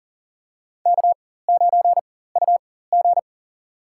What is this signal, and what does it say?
Morse code, heard as a keyed tone, about 40 wpm, spaced as for 19 wpm, 700 Hz. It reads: K9UG
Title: USA Call Signs - 4 Characters - 4x Character Spacing - 40wpm